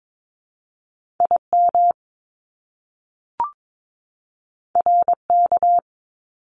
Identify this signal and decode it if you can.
Morse code, heard as a keyed tone, about 22 wpm, 700 Hz. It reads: IM RK
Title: Top Two Letter Combinations in Words 22wpm